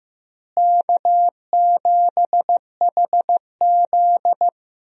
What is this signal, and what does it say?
Morse code, heard as a keyed tone, about 15 wpm, 700 Hz. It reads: K7HZ
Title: USA Call Signs - 4 Characters - Repeated 3x - 15wpm